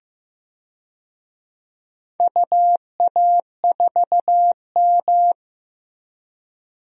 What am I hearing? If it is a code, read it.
Morse code, heard as a keyed tone, about 15 wpm, 700 Hz. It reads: UA4M
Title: DXCC-100 Call Signs - 4 Characters - Repeated 2x - 15wpm